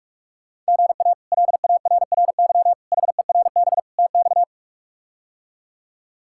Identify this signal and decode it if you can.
Morse code, heard as a keyed tone, about 46 wpm, 700 Hz. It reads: GA LARRY HERB TX
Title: K1USN SST QSO Practice - 45wpm